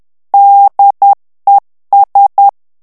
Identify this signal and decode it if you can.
Morse code, heard as a keyed tone, about 10 wpm, 800 Hz. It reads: DES
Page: Français - texte en clair 1 - vitesse : 10 mots minute